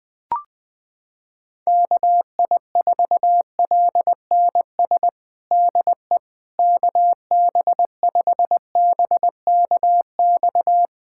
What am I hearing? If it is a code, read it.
Morse code, heard as a keyed tone, about 21 wpm, 700 Hz. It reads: KI4LNS DE KB5BKX
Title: USA Callsign DE Callsign 20wpm